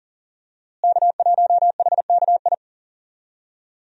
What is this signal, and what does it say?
Morse code, heard as a keyed tone, about 40 wpm, 700 Hz. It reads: K1HKI
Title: USA Call Signs - 5 Characters - Repeated 2x - 40wpm